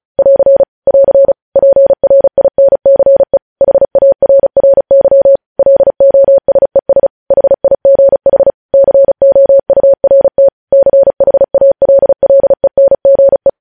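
Morse code, heard as a keyed tone, about 36 wpm, 550 Hz. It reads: <AR> <AR> PRINCE HARRY LOSES HIGH COURT CHALLENGE
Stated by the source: Morse news headlines practice at 35 words per minute.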